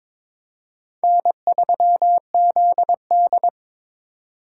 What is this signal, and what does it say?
Morse code, heard as a keyed tone, about 22 wpm, 700 Hz. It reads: N3ZD
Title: USA Call Signs - 3x Character Spacing - 22wpm